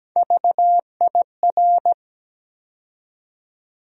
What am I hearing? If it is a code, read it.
Morse code, heard as a keyed tone, about 17 wpm, 700 Hz. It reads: VIR
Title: USA Call Sign Postfixes - 3 Letters 17wpm